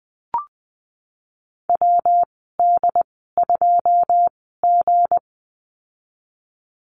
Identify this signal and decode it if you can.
Morse code, heard as a keyed tone, about 20 wpm, spaced as for 14 wpm, 700 Hz. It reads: WD2G
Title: 193 Lesson 31g - Letter v - Call Signs 20wpm